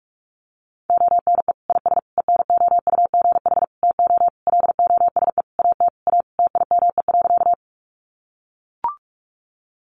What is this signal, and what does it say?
Morse code, heard as a keyed tone, about 44 wpm, 700 Hz. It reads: ONE IS ENOUGH TO LOSE AT A TIME.
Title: Sentences from Top 1100 Words 45wpm